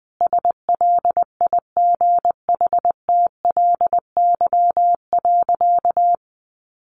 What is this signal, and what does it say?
Morse code, heard as a keyed tone, about 19 wpm, 700 Hz. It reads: SLIGHTLY.
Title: Sentences from Top 1300 Words 20wpm